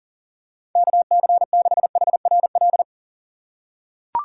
Morse code, heard as a keyed tone, about 40 wpm, 700 Hz. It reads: KC6HRL
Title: USA Call Signs - 6 Characters - 2x Character Spacing - 40wpm